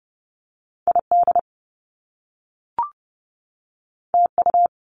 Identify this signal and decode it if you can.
Morse code, heard as a keyed tone, about 30 wpm, 700 Hz. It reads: ID TU
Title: Top Two Letter Combinations in Words 30wpm